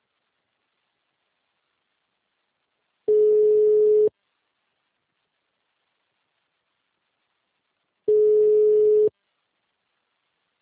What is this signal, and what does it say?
Ringback tone (Europe, 425 Hz).